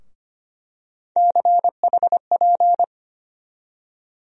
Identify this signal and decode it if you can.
Morse code, heard as a keyed tone, about 25 wpm, 700 Hz. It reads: CHP
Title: USA Call Sign Postfixes - 3 Letters 25wpm